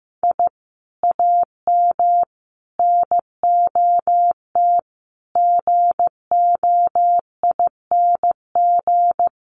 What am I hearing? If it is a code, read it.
Morse code, heard as a keyed tone, about 15 wpm, 700 Hz. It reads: I AM NOT GOING
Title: Sets of 5 Words Encore 15wpm